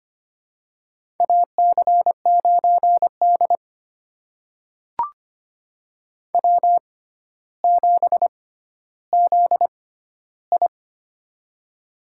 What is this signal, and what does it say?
Morse code, heard as a keyed tone, about 25 wpm, spaced as for 13 wpm, 700 Hz. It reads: AC9D W7ZI
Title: USA Call Signs - 4 Characters - 6x Character Spacing - 25wpm